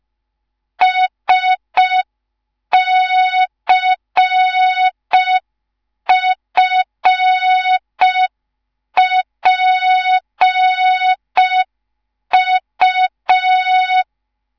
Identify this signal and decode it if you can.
Morse code, heard as a keyed tone, about 5 wpm, 3700 Hz. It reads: SCFPU
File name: Morze-HA2MN / hang / 10GWZ HA2MN